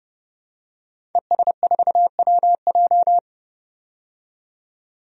Morse code, heard as a keyed tone, about 30 wpm, 700 Hz. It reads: ES4WJ
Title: DXCC-100 Call Signs - 5 Characters - 30wpm